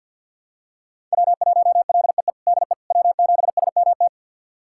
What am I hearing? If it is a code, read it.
Morse code, heard as a keyed tone, about 46 wpm, 700 Hz. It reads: W1LEE DE W7SMT
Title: USA Callsign DE Callsign - 5 Characters - 50wpm